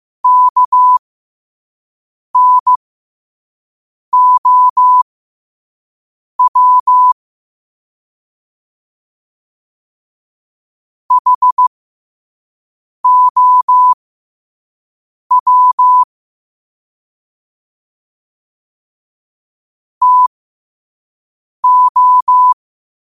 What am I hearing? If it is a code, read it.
Morse code, heard as a keyed tone, about 15 wpm, spaced as for 5 wpm, 1000 Hz. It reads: KNOW HOW TO
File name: qotdfiles / 5wpm